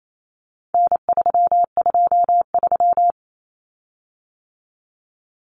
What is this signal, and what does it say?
Morse code, heard as a keyed tone, about 28 wpm, 700 Hz. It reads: N323
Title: Common Cut Numbers - 4-Digit 28wpm